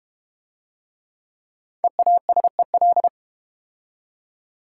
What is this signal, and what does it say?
Morse code, heard as a keyed tone, about 32 wpm, 700 Hz. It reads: EASEL